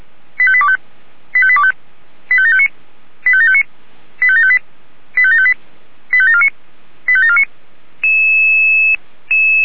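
Five-tone selective call (ZVEI), heard as quick runs of five tones, tones of about 70 ms. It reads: 85825 85825 85759 85759 85758 85758 85749 85749
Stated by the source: ZVEI five-tone selective call received with an SDR.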